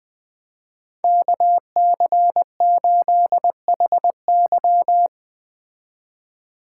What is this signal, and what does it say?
Morse code, heard as a keyed tone, about 19 wpm, 700 Hz. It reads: KC8HY
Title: Speed Racing - USA 2x2 Call Signs 17wpm